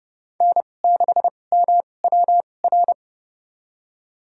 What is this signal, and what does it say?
Morse code, heard as a keyed tone, about 30 wpm, spaced as for 22 wpm, 700 Hz. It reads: N6MWR
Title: USA Call Signs - 5 Characters - 2x Character Spacing - 30wpm